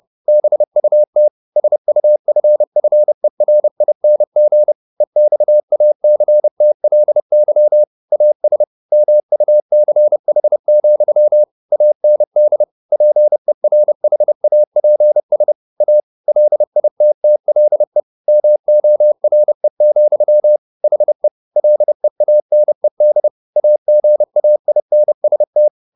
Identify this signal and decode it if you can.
Morse code, heard as a keyed tone, about 30 wpm, 600 Hz. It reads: BUT SUFFERING EXACTLY AS MUCH, AND PERHAPS A LITTLE MORE, HE LEANED AGAINST
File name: the old man and the sea / the old man and the sea 1 30wpm